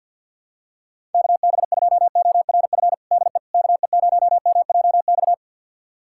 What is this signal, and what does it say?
Morse code, heard as a keyed tone, about 49 wpm, 700 Hz. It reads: KB2ORF DE KE0MJX
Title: USA Callsign DE Callsign - 6 Characters - 50wpm